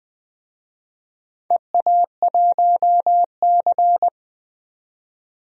Morse code, heard as a keyed tone, about 20 wpm, 700 Hz. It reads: EA1C
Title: DXCC-100 Call Signs - 4 Characters - 20wpm